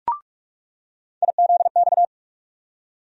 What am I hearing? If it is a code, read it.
Morse code, heard as a keyed tone, about 44 wpm, 700 Hz. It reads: IZX
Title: USA Call Sign Postfixes - 3 Letters 45wpm